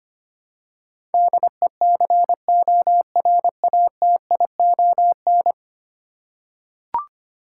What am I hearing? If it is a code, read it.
Morse code, heard as a keyed tone, about 25 wpm, 700 Hz. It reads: DECORATION
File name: Ten Letter Words - 25wpm